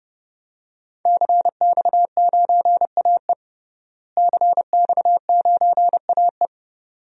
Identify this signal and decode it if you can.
Morse code, heard as a keyed tone, about 30 wpm, 700 Hz. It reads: CX9AE CX9AE
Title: DXCC-100 Call Signs - 5 Characters - Repeated 2x - 30wpm